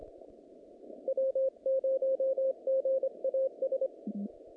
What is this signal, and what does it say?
Morse code, heard as a keyed tone, about 25 wpm, 530 Hz. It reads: W0GAS
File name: contests / naqp-cw-k5tr-op-k5pi- -2200